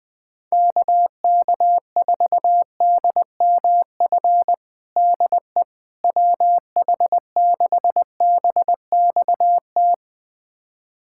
Morse code, heard as a keyed tone, about 20 wpm, 700 Hz. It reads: KK4DMF DE WH6BXT
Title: USA Callsign DE Callsign 20wpm